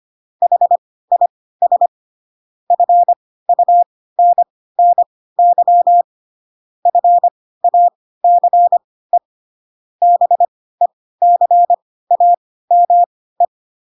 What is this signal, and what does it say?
Morse code, heard as a keyed tone, about 25 wpm, spaced as for 16 wpm, 700 Hz. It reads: HIS FUNNY FACE BECAME
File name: Pinocchio / Pinocchio ch1 10 25 ik1hge